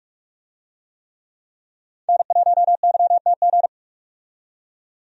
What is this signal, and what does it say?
Morse code, heard as a keyed tone, about 44 wpm, 700 Hz. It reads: N1YTG
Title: USA Call Signs - 5 Characters - 45wpm